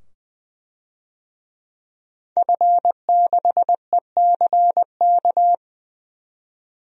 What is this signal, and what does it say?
Morse code, heard as a keyed tone, about 20 wpm, 700 Hz. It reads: F6ECK
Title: DXCC-100 Call Signs - 5 Characters - 20wpm